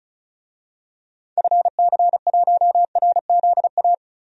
Morse code, heard as a keyed tone, about 36 wpm, 700 Hz. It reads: FC1RZA
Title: DXCC-100 Call Signs - 6 Characters - Repeated 2x - 35wpm